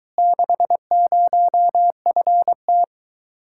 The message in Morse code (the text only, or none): 60FT